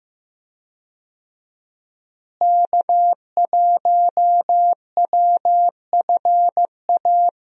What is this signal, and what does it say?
Morse code, heard as a keyed tone, about 15 wpm, 700 Hz. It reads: K1WFA
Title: USA Call Signs - 5 Characters - Repeated 2x - 15wpm